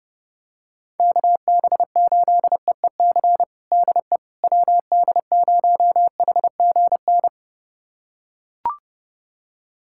Morse code, heard as a keyed tone, about 30 wpm, 700 Hz. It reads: KB8EEC DE WD0HGN
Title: USA Callsign DE Callsign - 6 Characters - 30wpm